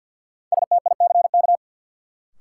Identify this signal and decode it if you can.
Morse code, heard as a keyed tone, about 51 wpm, 700 Hz. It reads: STICK
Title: Sentences from Top 1200 Words - New Word Drill - 50wpm